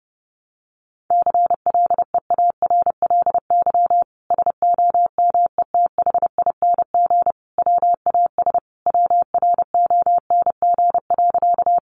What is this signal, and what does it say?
Morse code, heard as a keyed tone, about 30 wpm, 700 Hz. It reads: CLEARLY SOMETHING WAS WRONG.
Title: Sentences from Top 1300 Words 30wpm